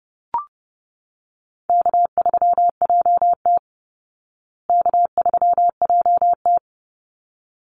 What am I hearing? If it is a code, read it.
Morse code, heard as a keyed tone, about 30 wpm, 700 Hz. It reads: K3JT K3JT
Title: Easy Call Signs Repeated 2x - 30wpm